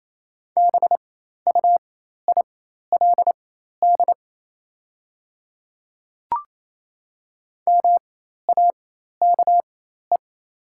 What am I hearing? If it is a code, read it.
Morse code, heard as a keyed tone, about 28 wpm, spaced as for 13 wpm, 700 Hz. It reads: BUILD MAKE
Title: 4x Character Spacing - Top 200 Words - 28wpm